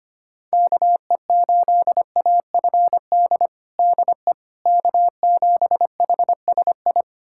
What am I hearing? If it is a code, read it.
Morse code, heard as a keyed tone, about 25 wpm, 700 Hz. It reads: KE8AFD DE K7HSI
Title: USA Callsign DE Callsign 25wpm